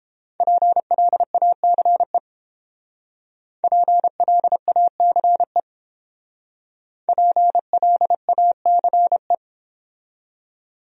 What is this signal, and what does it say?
Morse code, heard as a keyed tone, about 29 wpm, 700 Hz. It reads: PLACE PLACE PLACE